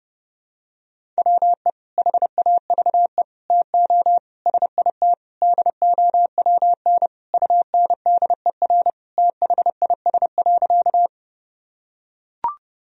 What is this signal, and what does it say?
Morse code, heard as a keyed tone, about 29 wpm, 700 Hz. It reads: WE HAVE TO SIT DOWN UNDER THIS.
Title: Sentences from Top 1000 Words - Review Part 1 - 30wpm